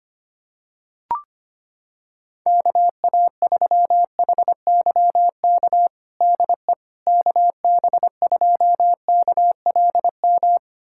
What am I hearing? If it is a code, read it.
Morse code, heard as a keyed tone, about 25 wpm, 700 Hz. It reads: KA3HYK DE KB2KLM
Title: USA Callsign DE Callsign - 6 Characters - 25wpm